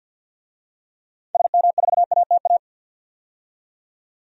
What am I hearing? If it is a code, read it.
Morse code, heard as a keyed tone, about 48 wpm, 700 Hz. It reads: SM4ATA
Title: DXCC-100 Call Signs - 6 Characters - 50wpm